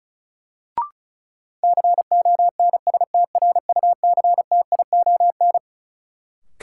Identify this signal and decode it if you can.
Morse code, heard as a keyed tone, about 36 wpm, 700 Hz. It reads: CONSTRUCTION
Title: Sentences from Top 1600 Words - New Word Drill - 35wpm